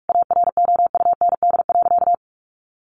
Morse code, heard as a keyed tone, about 44 wpm, 700 Hz. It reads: AROUND.
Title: Sentences from Top 600 Words 45wpm